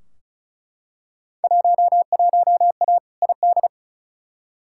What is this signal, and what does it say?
Morse code, heard as a keyed tone, about 36 wpm, 690 Hz. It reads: 11A ID
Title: Field Day Exchanges Encore - 35wpm